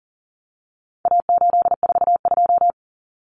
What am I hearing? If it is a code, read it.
Morse code, heard as a keyed tone, about 40 wpm, 700 Hz. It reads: A842
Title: Common Cut Numbers - 4-Digit 40wpm